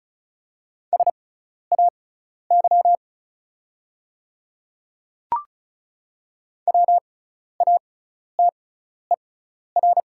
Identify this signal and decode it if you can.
Morse code, heard as a keyed tone, about 34 wpm, spaced as for 12 wpm, 700 Hz. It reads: SAY WATER
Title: 6x Character Spacing - Top 200 Words - 35wpm